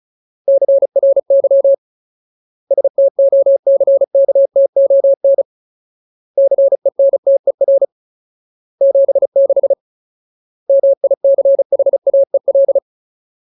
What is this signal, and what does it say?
Morse code, heard as a keyed tone, about 34 wpm, 550 Hz. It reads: CRY STOCKTON CENTER 76 MICHAEL
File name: Ultimate Word List / 35WPM Ultimate